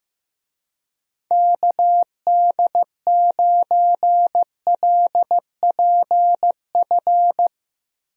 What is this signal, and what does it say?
Morse code, heard as a keyed tone, about 16 wpm, 700 Hz. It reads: KD9LPF
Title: USA Call Signs - 6 Characters - Repeated 2x - 15wpm